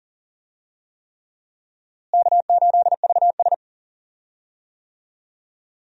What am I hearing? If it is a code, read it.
Morse code, heard as a keyed tone, about 38 wpm, 700 Hz. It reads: K8VS
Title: USA Call Signs - 4 Characters - Repeated 3x - 40wpm